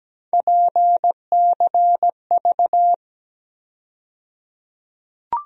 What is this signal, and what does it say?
Morse code, heard as a keyed tone, about 17 wpm, 700 Hz. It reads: PCV
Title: USA Call Sign Postfixes - 3 Letters 17wpm